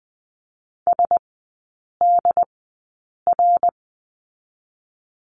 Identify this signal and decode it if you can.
Morse code, heard as a keyed tone, about 20 wpm, spaced as for 8 wpm, 700 Hz. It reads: SDR
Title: 067 Lesson 12e - Letter d - Three Characters 20wpm